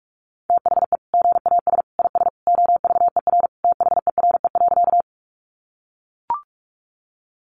Morse code, heard as a keyed tone, about 44 wpm, 700 Hz. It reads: THE GAS IS OVER THERE.
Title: Sentences from Top 1100 Words 45wpm